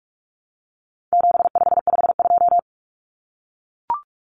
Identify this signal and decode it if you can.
Morse code, heard as a keyed tone, about 44 wpm, 700 Hz. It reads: 7552
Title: Common Cut Numbers - 4-Digit 45wpm